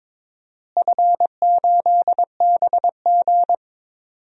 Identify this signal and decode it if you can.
Morse code, heard as a keyed tone, about 22 wpm, 700 Hz. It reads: F8BG
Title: DXCC-100 Call Signs - 4 Characters - Repeated 2x - 22wpm